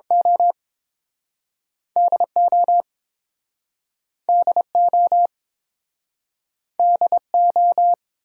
Morse code, heard as a keyed tone, about 30 wpm, spaced as for 6 wpm, 700 Hz. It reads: OD OD OD O